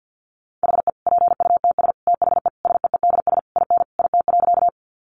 Morse code, heard as a keyed tone, about 50 wpm, 700 Hz. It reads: HE PUTS THE SEEDS IN IT.
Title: Easy Phrases 50wpm